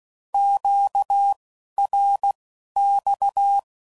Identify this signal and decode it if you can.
Morse code, heard as a keyed tone, about 16 wpm, spaced as for 11 wpm, 800 Hz. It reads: QRX
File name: CW Code Groups @10WPM / Q Calls / Q Calls Once